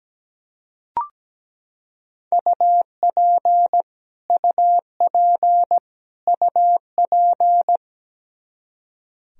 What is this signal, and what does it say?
Morse code, heard as a keyed tone, about 17 wpm, 700 Hz. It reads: UP UP UP
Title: Top 100 Words Repeated 17wpm